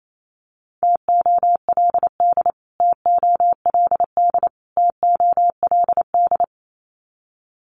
Morse code, heard as a keyed tone, about 28 wpm, 700 Hz. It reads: TOLD TOLD TOLD
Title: Top 500 Words Repeated 28wpm